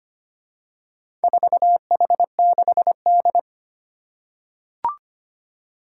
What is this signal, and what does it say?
Morse code, heard as a keyed tone, about 25 wpm, 700 Hz. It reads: 4H6D
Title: Four Challenging Letter-Number Combinations 25wpm